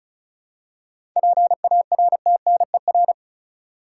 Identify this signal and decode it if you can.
Morse code, heard as a keyed tone, about 34 wpm, 700 Hz. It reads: PARTNER
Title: Seven Letter Words - 35wpm